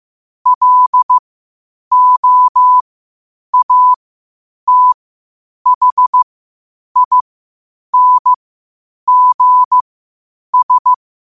Morse code, heard as a keyed tone, about 15 wpm, spaced as for 8 wpm, 1000 Hz. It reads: LOATHINGS
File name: qotdfiles / 8wpm